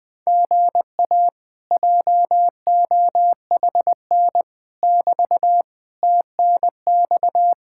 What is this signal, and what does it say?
Morse code, heard as a keyed tone, about 20 wpm, 700 Hz. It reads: GA JOHN <BT> TNX
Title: Speed-Racing Common QSO Phrases - 17wpm